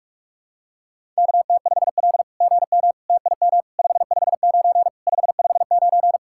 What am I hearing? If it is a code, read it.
Morse code, heard as a keyed tone, about 46 wpm, 690 Hz. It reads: KT5L GM TIM 559 559
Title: SOTA QSO Practice 45wpm